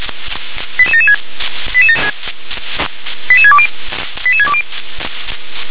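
Five-tone selective call (ZVEI), heard as quick runs of five tones, tones of about 70 ms. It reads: 88776 88620 88620
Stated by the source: ZVEI five-tone call 88773 received by SDR.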